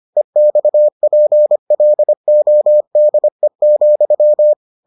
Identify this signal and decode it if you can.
Morse code, heard as a keyed tone, about 25 wpm, 600 Hz. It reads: EXPLODE,